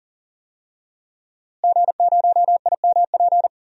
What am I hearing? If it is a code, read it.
Morse code, heard as a keyed tone, about 40 wpm, 700 Hz. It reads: G0IMP
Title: DXCC-100 Call Signs - 5 Characters - Repeated 2x - 40wpm